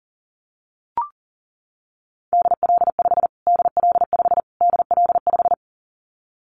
Morse code, heard as a keyed tone, about 40 wpm, 700 Hz. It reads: DL5 DL5 DL5
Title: Callsign Prefixes for Top 25 DXCC - Repeated 3x - 40wpm